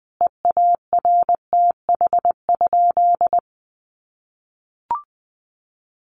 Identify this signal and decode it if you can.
Morse code, heard as a keyed tone, about 20 wpm, 700 Hz. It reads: EARTH?
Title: Sentences from Top 1500 Words 20wpm